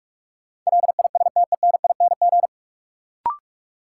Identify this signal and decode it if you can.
Morse code, heard as a keyed tone, about 46 wpm, 700 Hz. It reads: LISTENING
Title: Sets of 5 Words - New Word Drill - 45wpm